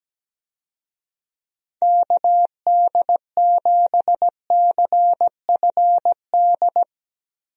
Morse code, heard as a keyed tone, about 17 wpm, 700 Hz. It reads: KD7CFD
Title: USA Call Signs - 6 Characters - Repeated 3x - 17wpm